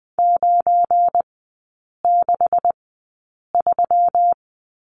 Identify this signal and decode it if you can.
Morse code, heard as a keyed tone, about 20 wpm, spaced as for 8 wpm, 700 Hz. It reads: 963
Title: 172 Lesson 28e - Number 9 - Three Characters 20wpm